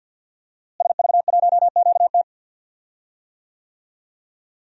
Morse code, heard as a keyed tone, about 46 wpm, 700 Hz. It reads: SV1QT
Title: DXCC-100 Call Signs - 5 Characters - Repeated 3x - 50wpm